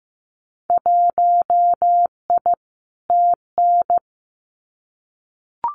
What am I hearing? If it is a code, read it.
Morse code, heard as a keyed tone, about 15 wpm, 700 Hz. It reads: 1I TN
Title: Winter Field Day Exchanges Encore 15wpm